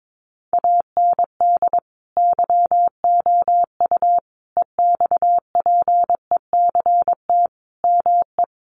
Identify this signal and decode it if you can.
Morse code, heard as a keyed tone, about 22 wpm, 700 Hz. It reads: AND YOU EXPECT ME
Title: Sentences from Top 1100 Words 22wpm